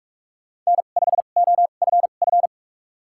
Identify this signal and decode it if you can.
Morse code, heard as a keyed tone, about 41 wpm, spaced as for 31 wpm, 700 Hz. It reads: N5OFF
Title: USA Call Signs - 5 Characters - 2x Character Spacing - 45wpm